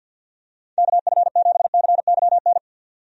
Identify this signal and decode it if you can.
Morse code, heard as a keyed tone, about 53 wpm, 700 Hz. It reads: KF7CYN
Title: USA Call Signs - 6 Characters - 50wpm